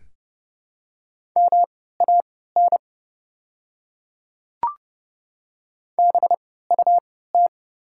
Morse code, heard as a keyed tone, about 30 wpm, spaced as for 17 wpm, 700 Hz. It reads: MAN BUT
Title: 3x Character Spacing - Top 300 Words - 30wpm